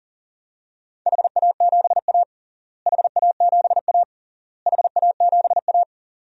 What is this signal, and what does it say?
Morse code, heard as a keyed tone, about 40 wpm, 700 Hz. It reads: HA7A HA7A HA7A
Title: DXCC-100 Call Signs - 4 Characters - Repeated 3x - 40wpm